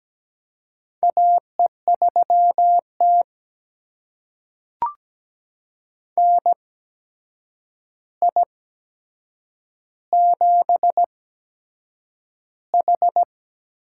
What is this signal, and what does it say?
Morse code, heard as a keyed tone, about 17 wpm, spaced as for 7 wpm, 700 Hz. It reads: AE3T NI7H
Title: USA Call Signs - 4 Characters - 8x Character Spacing - 17wpm